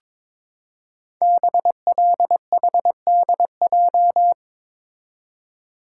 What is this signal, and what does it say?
Morse code, heard as a keyed tone, about 23 wpm, 690 Hz. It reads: BLHDJ
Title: Code Groups - 5 Letters - 22wpm